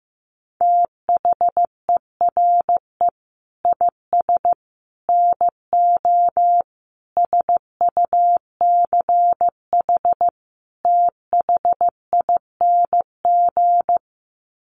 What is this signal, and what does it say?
Morse code, heard as a keyed tone, about 15 wpm, 700 Hz. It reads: THERE IS NO SUCH THING
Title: Sets of 5 Words Encore 15wpm